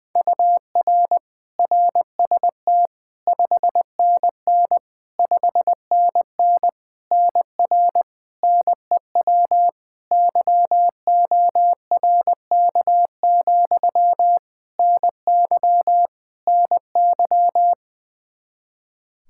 Morse code, heard as a keyed tone, about 20 wpm, 700 Hz. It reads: UR RST 5NN 5NN NR NEW YORK, NY NY
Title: Practice QSO - Exchange 1 - Beginning - 20wpm